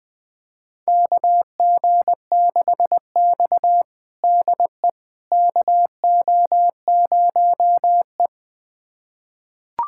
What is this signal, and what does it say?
Morse code, heard as a keyed tone, about 20 wpm, 700 Hz. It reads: KG6X DE KO0E
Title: USA Callsign DE Callsign - 4 Characters - 20wpm